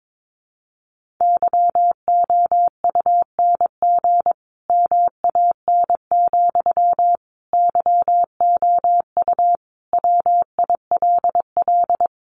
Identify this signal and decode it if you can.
Morse code, heard as a keyed tone, about 22 wpm, 700 Hz. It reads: YOUNG MAN, YOU WILL
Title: Sentences from Top 400 Words 22wpm